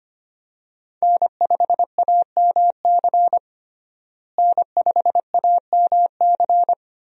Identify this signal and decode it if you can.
Morse code, heard as a keyed tone, about 25 wpm, 690 Hz. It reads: N5AMC N5AMC
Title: USA Call Signs - 5 Characters - Repeated 2x - 25wpm